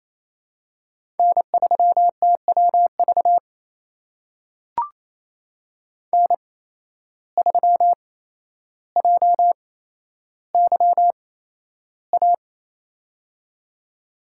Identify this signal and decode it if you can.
Morse code, heard as a keyed tone, about 28 wpm, spaced as for 12 wpm, 700 Hz. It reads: N3TWV N3JYA
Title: USA Call Signs - 5 Characters - 8x Character Spacing - 28wpm